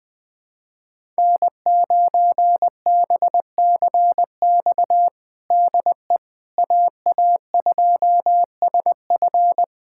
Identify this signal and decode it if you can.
Morse code, heard as a keyed tone, about 20 wpm, 700 Hz. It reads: N9BCX DE AA2SF
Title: USA Callsign DE Callsign - 5 Characters - 20wpm